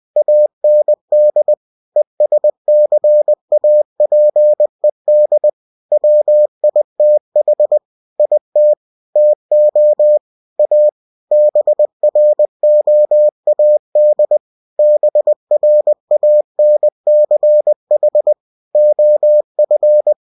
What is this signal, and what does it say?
Morse code, heard as a keyed tone, about 20 wpm, 600 Hz. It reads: AND ESCAPED WITH IT TO A BROAD BRANCH OF
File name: cw-books / winnie-the-pooh / winnie-20wpm-page-0094